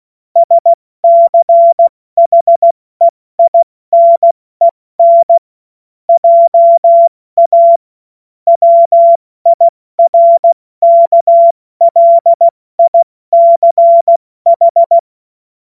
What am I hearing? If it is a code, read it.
Morse code, heard as a keyed tone, about 16 wpm, spaced as for 14 wpm, 680 Hz. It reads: SCHEINEN JA WIRKLICH
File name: Witz-16-September-2025-70-F80